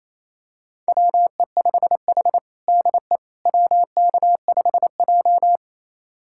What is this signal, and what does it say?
Morse code, heard as a keyed tone, about 28 wpm, 700 Hz. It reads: WE5H DE WK5J
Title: USA Callsign DE Callsign - 4 Characters - 28wpm